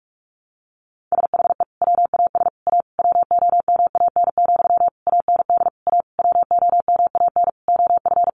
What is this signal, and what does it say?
Morse code, heard as a keyed tone, about 44 wpm, 700 Hz. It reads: SHE WAS A WOMAN, AND A WOMAN OF
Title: Sentences from Top 1400 Words 45wpm